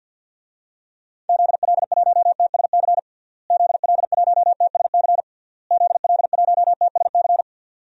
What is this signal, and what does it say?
Morse code, heard as a keyed tone, about 47 wpm, 700 Hz. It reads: ZL1TSC ZL1TSC ZL1TSC
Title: DXCC-100 Call Signs - 6 Characters - Repeated 3x - 50wpm